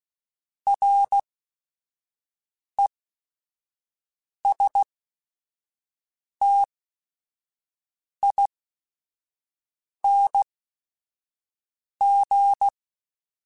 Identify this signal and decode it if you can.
Morse code, heard as a keyed tone, about 16 wpm, spaced as for 5 wpm, 800 Hz. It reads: RESTING